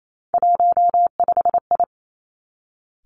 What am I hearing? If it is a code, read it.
Morse code, heard as a keyed tone, about 28 wpm, 700 Hz. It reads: 15I
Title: Three Challenging Letter-Number Combinations 28wpm